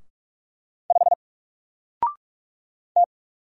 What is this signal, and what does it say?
Morse code, heard as a keyed tone, about 44 wpm, 700 Hz. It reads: 5 T
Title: Single Letter-Number Alt 45wpm